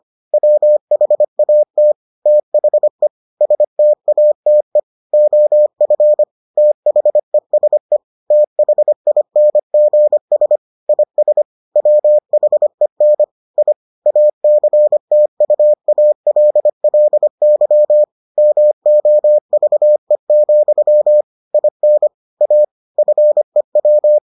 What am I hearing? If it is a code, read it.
Morse code, heard as a keyed tone, about 25 wpm, 600 Hz. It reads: WHAT THE STATE OF THESE THINGS IS WHEN I ACTUALLY MOVE, IN A FEW